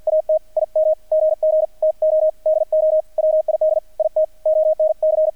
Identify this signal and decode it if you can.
Morse code, heard as a keyed tone, about 48 wpm, 620 Hz, with radio noise. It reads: AT IM GG TO DO WID IT ONC
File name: archived / CRSnet